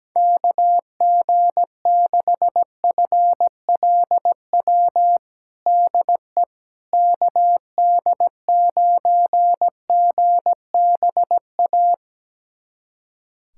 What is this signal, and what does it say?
Morse code, heard as a keyed tone, about 17 wpm, 700 Hz. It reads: KG6FLW DE KD9GBA
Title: USA Callsign DE Callsign 17wpm